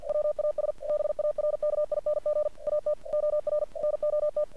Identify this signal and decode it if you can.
Morse code, heard as a keyed tone, about 48 wpm, 600 Hz. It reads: WAS BARKING AT OR NOT
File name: archived / CRSnet